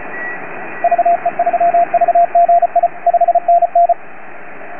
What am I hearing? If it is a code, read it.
Morse code, heard as a keyed tone, about 36 wpm, 690 Hz, with radio noise. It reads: VE3VGI 5NN